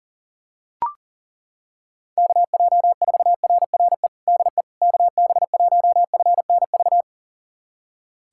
Morse code, heard as a keyed tone, about 40 wpm, 700 Hz. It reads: KJ4RRE DE KB1FNV
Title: USA Callsign DE Callsign - 6 Characters - 40wpm